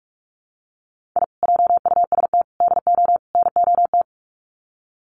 Morse code, heard as a keyed tone, about 44 wpm, 700 Hz. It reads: I JUST DO NOT
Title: Sets of 4 Words 45wpm